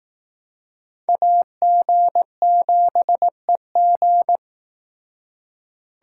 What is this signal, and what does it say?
Morse code, heard as a keyed tone, about 19 wpm, 700 Hz. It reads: AG7EG
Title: Speed Racing - USA 2x2 Call Signs 15wpm